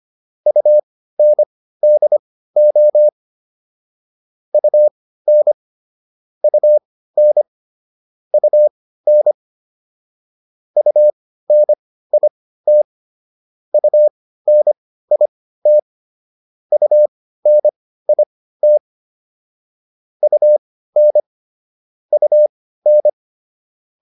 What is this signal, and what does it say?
Morse code, heard as a keyed tone, about 25 wpm, spaced as for 15 wpm, 600 Hz. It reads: UNDO UN UN UN UNIT UNIT UNIT UN UN